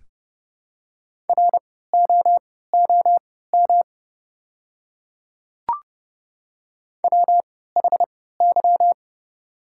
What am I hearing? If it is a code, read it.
Morse code, heard as a keyed tone, about 30 wpm, spaced as for 17 wpm, 700 Hz. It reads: ROOM WHY
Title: 3x Character Spacing - Top 300 Words - 30wpm